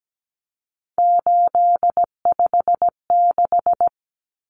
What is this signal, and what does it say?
Morse code, heard as a keyed tone, about 17 wpm, 700 Hz. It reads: 856
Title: Numbers - 3-Digit 17wpm